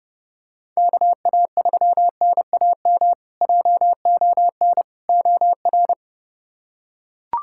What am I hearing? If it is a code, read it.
Morse code, heard as a keyed tone, about 28 wpm, 700 Hz. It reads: KA3NAM JON OR
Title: K1USN SST QSO Practice - 30wpm